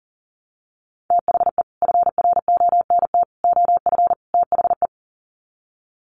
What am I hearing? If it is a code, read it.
Morse code, heard as a keyed tone, about 40 wpm, 700 Hz. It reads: THE FRONT OF THE
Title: Sets of 4 Words 40wpm